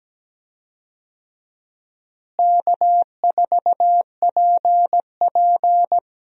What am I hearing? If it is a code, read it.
Morse code, heard as a keyed tone, about 17 wpm, 700 Hz. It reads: K4PP
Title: USA Call Signs - 4 Characters - Repeated 3x - 17wpm